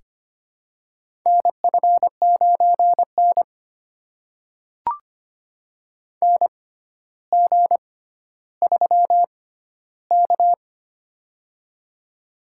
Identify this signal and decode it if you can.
Morse code, heard as a keyed tone, about 25 wpm, spaced as for 13 wpm, 700 Hz. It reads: NF9N NG3K
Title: USA Call Signs - 4 Characters - 6x Character Spacing - 25wpm